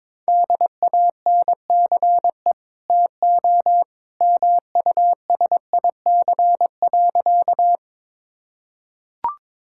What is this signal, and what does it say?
Morse code, heard as a keyed tone, about 22 wpm, 700 Hz. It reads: DANCE TO MUSIC.
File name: practice / 22wpm / Easy Sentences with Hints - 22wpm